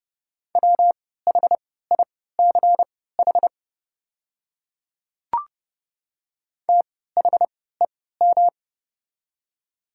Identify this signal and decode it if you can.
Morse code, heard as a keyed tone, about 32 wpm, spaced as for 17 wpm, 700 Hz. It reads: WHICH THEM
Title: 3x Character Spacing - Top 300 Words - 30wpm